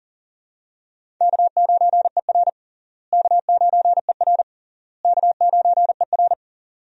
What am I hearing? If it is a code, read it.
Morse code, heard as a keyed tone, about 37 wpm, 700 Hz. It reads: K9ER K9ER K9ER
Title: USA Call Signs - 4 Characters - Repeated 3x - 40wpm